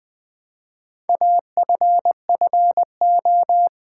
Morse code, heard as a keyed tone, about 20 wpm, 700 Hz. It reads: AFFO
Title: Build A Word - 2 Letters At A Time - 6-Letter Words - 20wpm